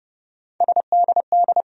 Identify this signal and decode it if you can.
Morse code, heard as a keyed tone, about 30 wpm, 700 Hz. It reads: SDD
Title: USA Call Sign Postfixes - 3 Letters 30wpm